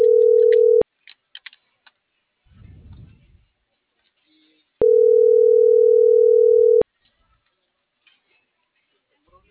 Ringback tone (North America), 440 + 480 Hz.